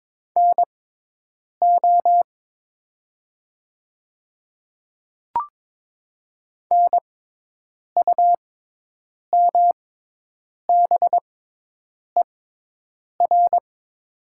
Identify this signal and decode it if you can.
Morse code, heard as a keyed tone, about 22 wpm, spaced as for 8 wpm, 700 Hz. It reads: NO NUMBER
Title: 6x Character Spacing - Top 200 Words - 22wpm